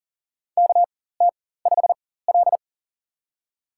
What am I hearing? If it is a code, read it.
Morse code, heard as a keyed tone, about 40 wpm, spaced as for 19 wpm, 700 Hz. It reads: KT5L
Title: USA Call Signs - 4 Characters - 4x Character Spacing - 40wpm